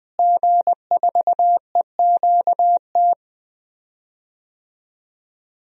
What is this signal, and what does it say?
Morse code, heard as a keyed tone, about 20 wpm, 700 Hz. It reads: G4EQT
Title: DXCC-100 Call Signs - 5 Characters - Repeated 2x - 20wpm